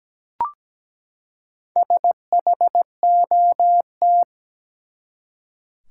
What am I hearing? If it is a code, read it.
Morse code, heard as a keyed tone, about 17 wpm, 700 Hz. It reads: SHOT
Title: Four Letter Words - 17wpm